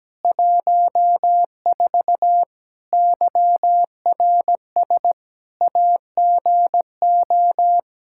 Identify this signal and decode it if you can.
Morse code, heard as a keyed tone, about 17 wpm, 700 Hz. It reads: 14 YRS AGO
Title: Practice QSO - Exchange 3 - More Info - 17wpm